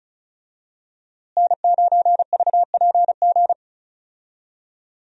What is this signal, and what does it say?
Morse code, heard as a keyed tone, about 36 wpm, 700 Hz. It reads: N9VPG
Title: USA Call Signs - 5 Characters - Repeated 2x - 35wpm